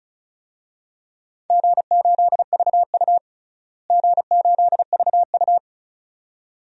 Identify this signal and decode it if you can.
Morse code, heard as a keyed tone, about 34 wpm, 700 Hz. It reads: G8VU G8VU
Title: DXCC-100 Call Signs - 4 Characters - Repeated 2x - 35wpm